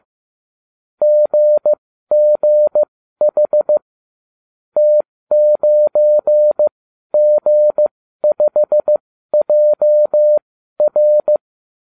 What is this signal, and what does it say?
Morse code, heard as a keyed tone, about 15 wpm, spaced as for 12 wpm, 600 Hz. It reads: GGH T9G5JR